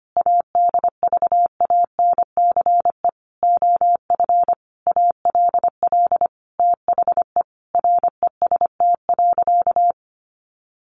Morse code, heard as a keyed tone, about 25 wpm, 700 Hz. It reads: ADVANCE OF ALL THE REST.
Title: Sentences from Top 1400 Words 25wpm